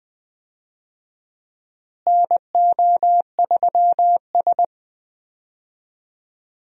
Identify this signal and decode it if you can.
Morse code, heard as a keyed tone, about 20 wpm, 700 Hz. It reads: NO3S
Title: USA Call Signs - 4 Characters - Repeated 3x - 20wpm